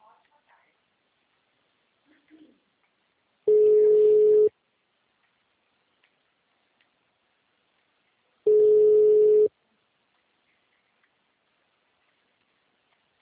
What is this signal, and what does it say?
Ringback tone (Europe, 425 Hz), with speech.
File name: out- -2014- - - .122